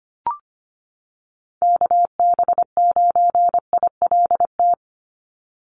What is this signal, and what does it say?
Morse code, heard as a keyed tone, about 25 wpm, 700 Hz. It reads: KB9ILT
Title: NA SOTA Activator Callsigns 25wpm